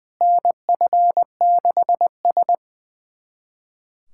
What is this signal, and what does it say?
Morse code, heard as a keyed tone, about 21 wpm, 700 Hz. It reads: NF6S
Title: USA Callsign DE Callsign - 4 Characters - 20wpm